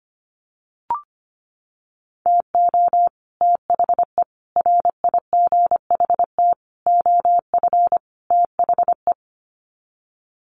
Morse code, heard as a keyed tone, about 25 wpm, 700 Hz. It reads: TO THE RIGHT OF THE
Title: Sets of 5 Words Encore 25wpm